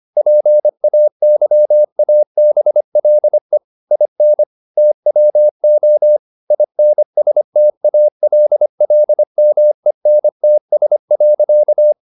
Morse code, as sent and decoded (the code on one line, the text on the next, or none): .--. .- -.-- .- -... .-.. / .. -. / - .-- --- / .. -. ... - .- .-.. .-.. -- . -. - ... .-.-.-
PAYABLE IN TWO INSTALLMENTS.